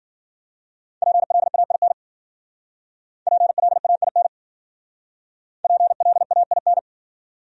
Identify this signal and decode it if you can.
Morse code, heard as a keyed tone, about 52 wpm, 700 Hz. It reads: PLAIN PLAIN PLAIN